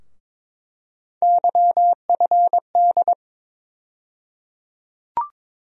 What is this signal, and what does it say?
Morse code, heard as a keyed tone, about 22 wpm, 700 Hz. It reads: YFD
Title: USA Call Sign Postfixes - 3 Letters 22wpm